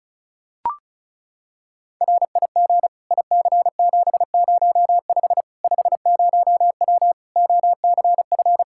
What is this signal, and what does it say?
Morse code, heard as a keyed tone, about 36 wpm, 700 Hz. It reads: RIG IC705 50W OCF
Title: Practice QSO - Exchange 2 - Info - 35wpm